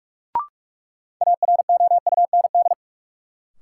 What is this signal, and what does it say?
Morse code, heard as a keyed tone, about 42 wpm, 700 Hz. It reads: AROUND
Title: Sets of 2 Words - New Word Drill - 45wpm